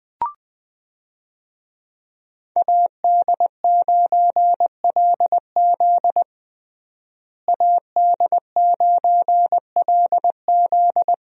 Morse code, heard as a keyed tone, about 20 wpm, 700 Hz. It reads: AD9LZ AD9LZ
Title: USA Call Signs - 5 Characters - Repeated 3x - 20wpm